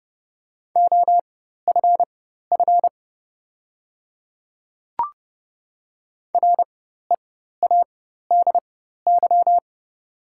Morse code, heard as a keyed tone, about 30 wpm, spaced as for 14 wpm, 690 Hz. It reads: OFF READY
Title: 4x Character Spacing - Top 500 Words - 30wpm